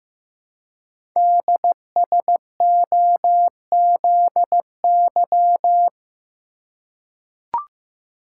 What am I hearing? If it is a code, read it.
Morse code, heard as a keyed tone, about 15 wpm, 700 Hz. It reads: DSOZY
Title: Code Groups - 5 Letters - 15wpm